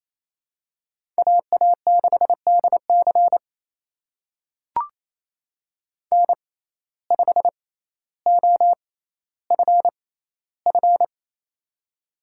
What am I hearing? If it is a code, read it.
Morse code, heard as a keyed tone, about 28 wpm, spaced as for 14 wpm, 700 Hz. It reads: AA6DC N5OFF
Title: USA Call Signs - 6x Character Spacing - 28wpm